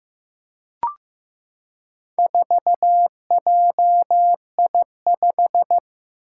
Morse code, heard as a keyed tone, about 15 wpm, 700 Hz. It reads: 4JI5
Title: Four Challenging Letter-Number Combinations 15wpm